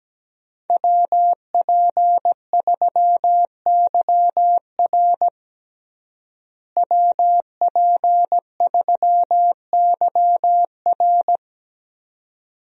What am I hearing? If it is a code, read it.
Morse code, heard as a keyed tone, about 17 wpm, 700 Hz. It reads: WP3YR WP3YR
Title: USA Call Signs - 5 Characters - Repeated 2x - 17wpm